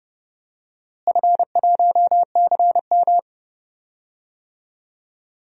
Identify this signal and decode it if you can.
Morse code, heard as a keyed tone, about 30 wpm, 700 Hz. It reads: F1CM
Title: DXCC-100 Call Signs - 4 Characters - Repeated 2x - 30wpm